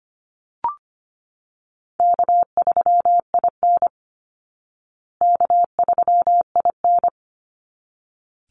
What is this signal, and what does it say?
Morse code, heard as a keyed tone, about 25 wpm, 700 Hz. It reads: K3IN K3IN
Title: Call Signs Repeated 3x - 25wpm